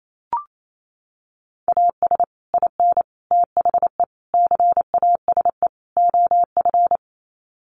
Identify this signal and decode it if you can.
Morse code, heard as a keyed tone, about 28 wpm, 700 Hz. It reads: AS IN THE CASE OF
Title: Sets of 5 Words 28wpm